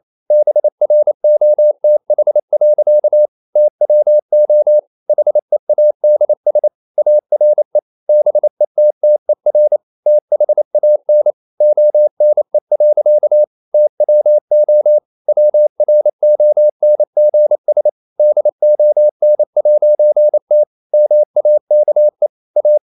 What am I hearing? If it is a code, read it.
Morse code, heard as a keyed tone, about 28 wpm, 600 Hz. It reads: BROTH. TWO HEADS ARE BETTER THAN ONE. TWO WRONGS DON'T MAKE A
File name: CW / Practice / EnglishSayings28